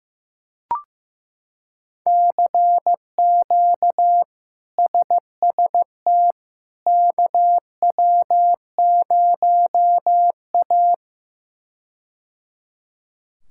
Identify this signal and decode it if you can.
Morse code, heard as a keyed tone, about 15 wpm, 700 Hz. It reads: CQ SST KW0A
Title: K1USN SST QSO Practice - 15wpm